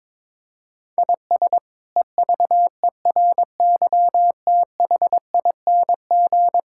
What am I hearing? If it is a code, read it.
Morse code, heard as a keyed tone, about 22 wpm, 700 Hz. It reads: IS EVERYTHING
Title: Sentences from Top 1000 Words - Review Part 6 - 22wpm